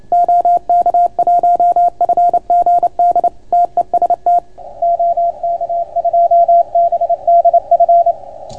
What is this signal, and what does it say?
Morse code, heard as a keyed tone, about 30 wpm, 690 Hz. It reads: OK1FGD TEST OK2BDF